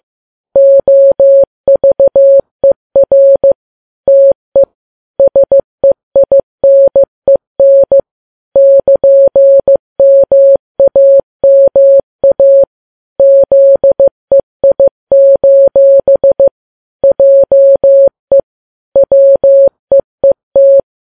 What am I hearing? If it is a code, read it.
Morse code, heard as a keyed tone, about 15 wpm, 550 Hz. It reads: OVER TE SEINEN <KN>MAMA ZEI: JE WEET
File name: Files Web tls / Morse on9fv / morse15kps